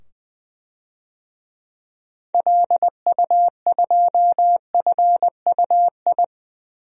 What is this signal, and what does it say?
Morse code, heard as a keyed tone, about 21 wpm, 700 Hz. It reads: LU2FUI
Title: DXCC-100 Call Signs - 6 Characters - 20wpm